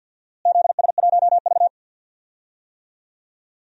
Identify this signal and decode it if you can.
Morse code, heard as a keyed tone, about 52 wpm, 700 Hz. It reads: ZS1V
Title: DXCC-100 Call Signs - 4 Characters - Repeated 2x - 50wpm